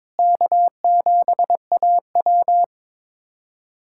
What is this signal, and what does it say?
Morse code, heard as a keyed tone, about 22 wpm, 700 Hz. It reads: K7AW
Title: USA Callsign DE Callsign - 4 Characters - 22wpm